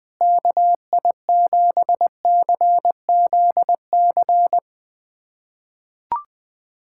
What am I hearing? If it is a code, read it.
Morse code, heard as a keyed tone, about 20 wpm, 700 Hz. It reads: KI7CZC
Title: USA Callsign DE Callsign - 6 Characters - 20wpm